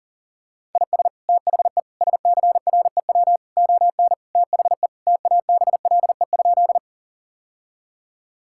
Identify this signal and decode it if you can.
Morse code, heard as a keyed tone, about 40 wpm, 700 Hz. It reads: IS THE SCREW ON THE TABLE?
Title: Sentences from Top 1600 Words 40wpm